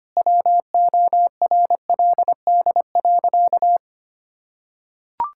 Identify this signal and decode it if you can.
Morse code, heard as a keyed tone, about 25 wpm, 700 Hz. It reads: WORLD.
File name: Sentences with 3-Word Phrases - 25wpm